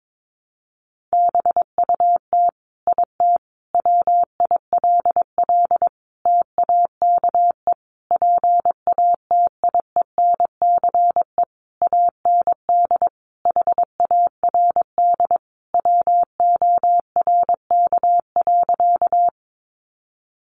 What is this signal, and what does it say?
Morse code, heard as a keyed tone, about 22 wpm, 700 Hz. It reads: BUT IT WILL TAKE PATIENCE AND HARD WORK.
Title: Sentences from Top 1300 Words 22wpm